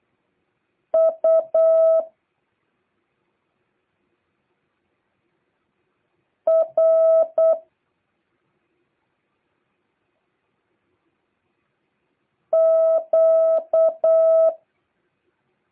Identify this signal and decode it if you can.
Morse code, heard as a keyed tone, about 8 wpm, 650 Hz. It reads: U R Q